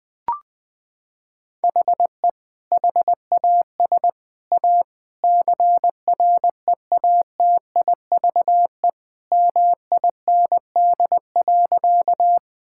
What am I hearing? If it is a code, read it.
Morse code, heard as a keyed tone, about 20 wpm, 700 Hz. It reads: HE HAS A CREATIVE MIND.
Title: Sentences from Top 1000 Words - Review Part 5 - 20wpm